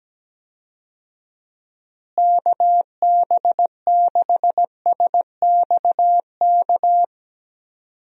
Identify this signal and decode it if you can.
Morse code, heard as a keyed tone, about 17 wpm, 700 Hz. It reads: KB6SXK
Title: USA Call Signs - 6 Characters - Repeated 2x - 17wpm